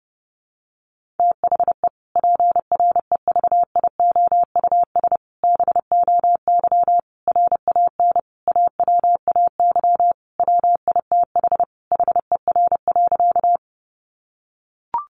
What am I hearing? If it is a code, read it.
Morse code, heard as a keyed tone, about 30 wpm, 700 Hz. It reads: THE PREVIOUS BOY RAN AWAY WITH HER.
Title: Sentences from Top 1500 Words 30wpm